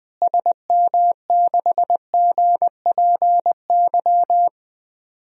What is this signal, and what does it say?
Morse code, heard as a keyed tone, about 20 wpm, 700 Hz. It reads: SM6GPY
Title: DXCC-100 Call Signs - 6 Characters - Repeated 3x - 20wpm